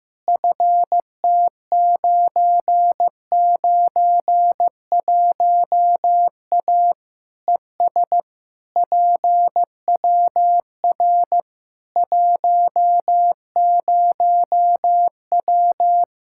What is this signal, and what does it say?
Morse code, heard as a keyed tone, about 15 wpm, 700 Hz. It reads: FT991A ES PWR 10W
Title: Practice QSO - Exchange 2 - Info - 15wpm